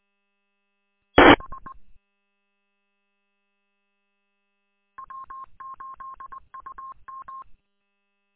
Morse code, heard as a keyed tone, about 20 wpm, 1070 Hz. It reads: B W8UM
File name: w8um- -combined